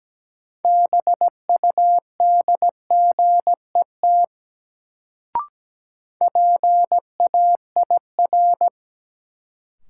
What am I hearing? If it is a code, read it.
Morse code, heard as a keyed tone, about 17 wpm, 700 Hz. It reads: BUDGET PAIR
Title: Sentences from Top 1200 Words - New Word Drill Alt - 17wpm